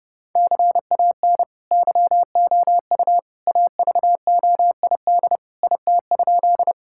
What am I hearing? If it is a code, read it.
Morse code, heard as a keyed tone, about 30 wpm, 700 Hz. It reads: CAN YOU AVOID IT?
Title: Sentences from Top 1000 Words - Review Part 6 - 30wpm